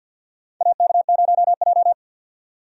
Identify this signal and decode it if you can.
Morse code, heard as a keyed tone, about 47 wpm, 700 Hz. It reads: AK0J
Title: USA Call Signs - 4 Characters - 50wpm